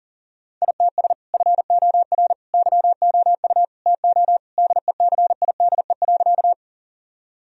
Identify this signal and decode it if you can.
Morse code, heard as a keyed tone, about 40 wpm, 700 Hz. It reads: ITS FOR YOU TO DECIDE.
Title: Sentences from Top 1000 Words - Review Part 3 - 40wpm